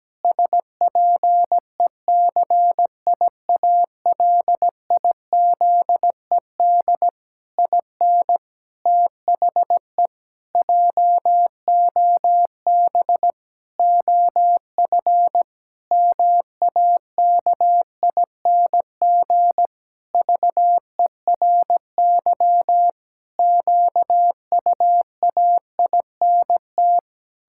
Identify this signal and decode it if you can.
Morse code, heard as a keyed tone, about 17 wpm, 700 Hz. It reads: SPECIALIZED IN THE JOB OF MAKING VERY QUAINT
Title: Pangram Challenge v2 - 17wpm